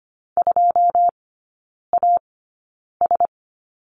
Morse code, heard as a keyed tone, about 25 wpm, spaced as for 9 wpm, 700 Hz. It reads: 2AS
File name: Lesson 14e - Number 2 - Three Characters 25wpm